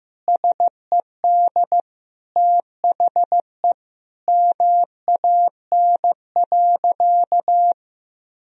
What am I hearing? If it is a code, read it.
Morse code, heard as a keyed tone, about 15 wpm, 700 Hz. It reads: SED THE MAN.
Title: Sentences from Top 400 Words and Abbreviations 15wpm